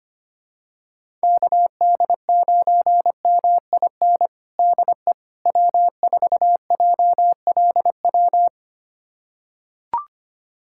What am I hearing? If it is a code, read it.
Morse code, heard as a keyed tone, about 25 wpm, 700 Hz. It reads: KD9MIN DE W4JLW
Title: USA Callsign DE Callsign 25wpm